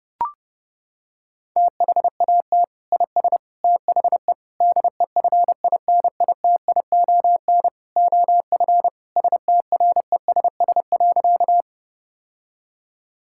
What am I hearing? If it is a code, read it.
Morse code, heard as a keyed tone, about 30 wpm, 700 Hz. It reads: THAT IS THE DEFINITION OF STRESS.
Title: Sentences from Top 1000 Words - Review Part 6 - 30wpm